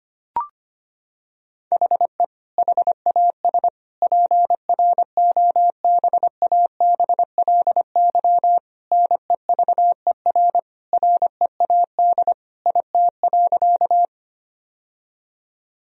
Morse code, heard as a keyed tone, about 25 wpm, 700 Hz. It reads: HE HAS PROBABLY NEVER READ IT.
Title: Sentences from Top 1000 Words - Review Part 4 - 25wpm